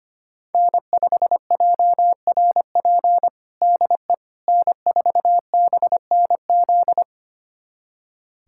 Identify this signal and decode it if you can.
Morse code, heard as a keyed tone, about 25 wpm, 700 Hz. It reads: N5JRP DE N4BNZ
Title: USA Callsign DE Callsign - 5 Characters - 25wpm